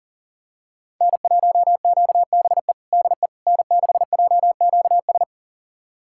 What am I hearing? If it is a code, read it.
Morse code, heard as a keyed tone, about 40 wpm, 700 Hz. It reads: N1QBE DE N6JQS
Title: USA Callsign DE Callsign - 5 Characters - 40wpm